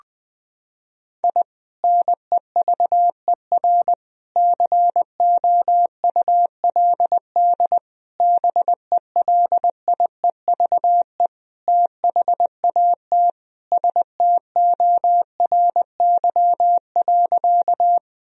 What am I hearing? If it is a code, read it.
Morse code, heard as a keyed tone, about 20 wpm, 700 Hz. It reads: I NEVER COULD BELIEVE THAT STORY.
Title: Sentences from Top 1000 Words - Review Part 1 - 20wpm